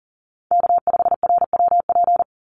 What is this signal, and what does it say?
Morse code, heard as a keyed tone, about 40 wpm, 700 Hz. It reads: K5RWP
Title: NA SOTA Chaser Callsigns 40wpm